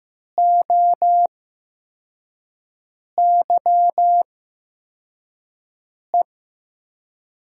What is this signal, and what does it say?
Morse code, heard as a keyed tone, about 15 wpm, spaced as for 4 wpm, 700 Hz. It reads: OYE
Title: USA Call Signs - 6 Characters - 8x Character Spacing - 15wpm